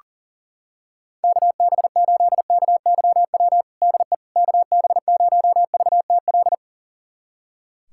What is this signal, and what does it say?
Morse code, heard as a keyed tone, about 40 wpm, 700 Hz. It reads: KB8KYW DE KB0VTL
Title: USA Callsign DE Callsign 40wpm